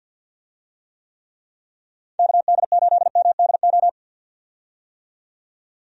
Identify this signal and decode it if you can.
Morse code, heard as a keyed tone, about 50 wpm, 700 Hz. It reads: KD8MDO
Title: USA Call Signs - 6 Characters - 50wpm